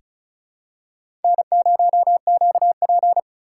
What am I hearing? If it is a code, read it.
Morse code, heard as a keyed tone, about 36 wpm, 700 Hz. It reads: N0QP
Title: USA Call Signs - 4 Characters - 6x Character Spacing - 35wpm